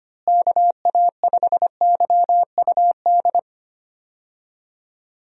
Morse code, heard as a keyed tone, about 25 wpm, 700 Hz. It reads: KA5YUD
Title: USA Call Signs - 6 Characters - Repeated 2x - 25wpm